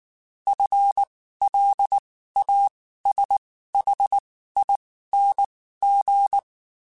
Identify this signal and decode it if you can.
Morse code, heard as a keyed tone, about 19 wpm, spaced as for 14 wpm, 800 Hz. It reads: FLASHING
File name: Spaced / FL Runs sp mp3s / FL Runs sp @18